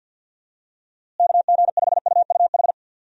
Morse code, heard as a keyed tone, about 46 wpm, 700 Hz. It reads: KG5UUH
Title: USA Call Signs - 6 Characters - Repeated 3x - 50wpm